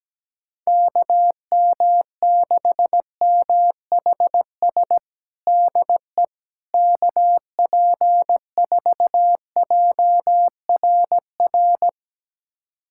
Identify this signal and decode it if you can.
Morse code, heard as a keyed tone, about 17 wpm, 700 Hz. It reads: KM6MHS DE KP4JRR
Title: USA Callsign DE Callsign - 6 Characters - 17wpm